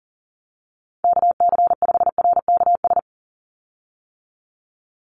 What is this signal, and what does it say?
Morse code, heard as a keyed tone, about 40 wpm, 700 Hz. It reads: KC5RKS
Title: USA Call Signs 40wpm